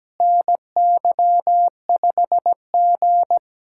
Morse code, heard as a keyed tone, about 17 wpm, 700 Hz. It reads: NY5G
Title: USA Callsign DE Callsign - 4 Characters - 17wpm